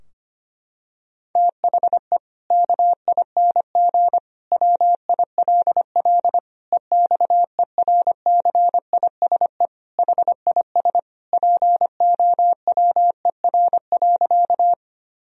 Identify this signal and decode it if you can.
Morse code, heard as a keyed tone, about 25 wpm, 700 Hz. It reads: THE KING WILL EXERCISE HIS POWER.
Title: Sentences from Top 1000 Words - Review Part 4 - 25wpm